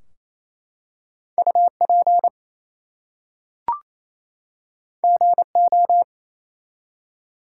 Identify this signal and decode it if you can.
Morse code, heard as a keyed tone, about 29 wpm, 700 Hz. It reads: UP GO
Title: Two Letter Words - 28wpm